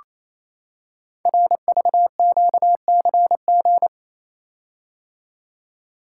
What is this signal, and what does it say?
Morse code, heard as a keyed tone, about 29 wpm, 700 Hz. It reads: RVQCG
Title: Code Groups - 5 Letters - 28wpm